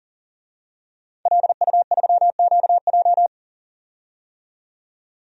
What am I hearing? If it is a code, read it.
Morse code, heard as a keyed tone, about 37 wpm, 700 Hz. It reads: LU3QJ
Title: DXCC-100 Call Signs - 5 Characters - 40wpm